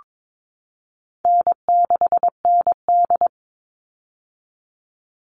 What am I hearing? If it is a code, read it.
Morse code, heard as a keyed tone, about 22 wpm, 700 Hz. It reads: N6ND
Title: All Cut Numbers - 4-Digit 22wpm